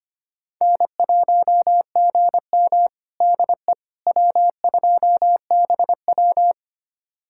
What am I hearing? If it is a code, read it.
Morse code, heard as a keyed tone, about 25 wpm, 700 Hz. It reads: N1GM DE W2BW
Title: USA Callsign DE Callsign - 4 Characters - 25wpm